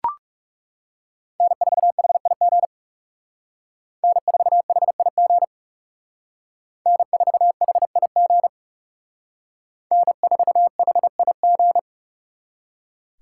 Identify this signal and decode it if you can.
Morse code, heard as a keyed tone, about 38 wpm, 700 Hz. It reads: N4HIG N4HIG N4HIG N4HIG